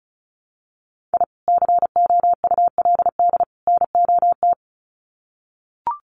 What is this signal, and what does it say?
Morse code, heard as a keyed tone, about 34 wpm, 700 Hz. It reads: I COULD NOT
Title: Sets of 3 Words 35wpm